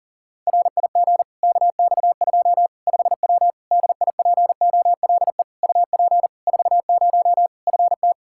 Morse code, heard as a keyed tone, about 37 wpm, 700 Hz. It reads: RIG KX2 5W DIPOLE UP 40 FT
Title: Practice QSO - Exchange 2 - Info - 40wpm